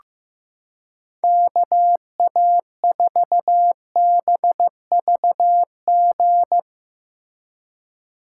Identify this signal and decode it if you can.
Morse code, heard as a keyed tone, about 15 wpm, 700 Hz. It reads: KA4BVG
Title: K1USN SST QSO Practice - 15wpm